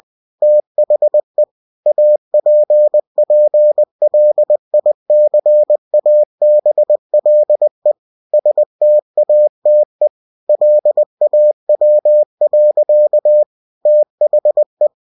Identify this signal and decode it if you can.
Morse code, heard as a keyed tone, about 20 wpm, 600 Hz. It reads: THE APPLICABLE STATE LAW. THE
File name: cw-books / winnie-the-pooh / winnie-20wpm-page-0119